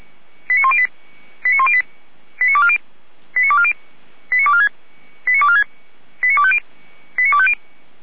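Five-tone selective call (ZVEI), heard as quick runs of five tones, tones of about 70 ms. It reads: 89108 89108 89240 89240 89246 89246 89250 89250
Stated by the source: ZVEI five-tone selective call received by SDR, from the Feuerwehr Orlamünde fire brigade audio server.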